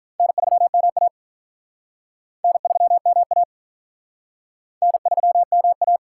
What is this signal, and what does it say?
Morse code, heard as a keyed tone, about 45 wpm, 700 Hz. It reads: N3MA N3MA N3MA